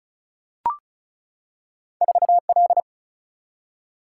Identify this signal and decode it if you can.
Morse code, heard as a keyed tone, about 36 wpm, 700 Hz. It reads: 4L
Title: Single-Letter Single-Number 35wpm